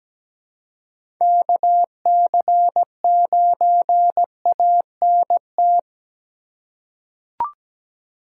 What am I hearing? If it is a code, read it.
Morse code, heard as a keyed tone, about 17 wpm, 700 Hz. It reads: KC9ANT
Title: USA Call Signs - 6 Characters - 2x Character Spacing - 17wpm